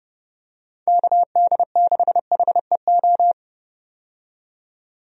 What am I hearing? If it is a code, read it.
Morse code, heard as a keyed tone, about 32 wpm, 700 Hz. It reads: KD6HEO
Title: USA Call Signs - 6 Characters - 30wpm